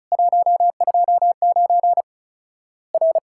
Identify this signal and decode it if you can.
Morse code, heard as a keyed tone, about 36 wpm, 700 Hz. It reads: 129 R
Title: Worked All Europe DX Contest - QTC Exchange - 35wpm